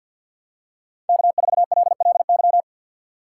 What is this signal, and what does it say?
Morse code, heard as a keyed tone, about 52 wpm, 700 Hz. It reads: K4LLY
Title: USA Call Signs - 6x Character Spacing - 50wpm